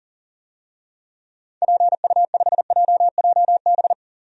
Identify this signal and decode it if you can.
Morse code, heard as a keyed tone, about 40 wpm, 700 Hz. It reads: PU5JJB
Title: DXCC-100 Call Signs - 6 Characters - Repeated 3x - 40wpm